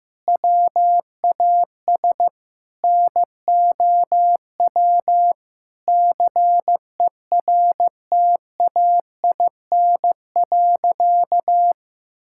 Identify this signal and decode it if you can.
Morse code, heard as a keyed tone, about 16 wpm, 700 Hz. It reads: WAS NOW CERTAIN.
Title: Sentences from Top 1000 Words - Review Part 5 - 15wpm